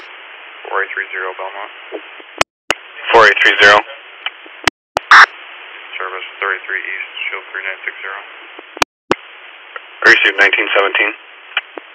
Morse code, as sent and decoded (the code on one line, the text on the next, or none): none